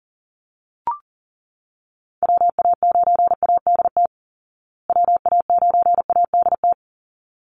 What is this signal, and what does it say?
Morse code, heard as a keyed tone, about 37 wpm, 700 Hz. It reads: WA9ADT WA9ADT
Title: Call Signs Repeated 3x - 40wpm